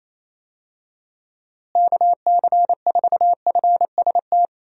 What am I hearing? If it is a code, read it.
Morse code, heard as a keyed tone, about 28 wpm, 700 Hz. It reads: KC4FST
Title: USA Call Signs - 6 Characters - Repeated 2x - 28wpm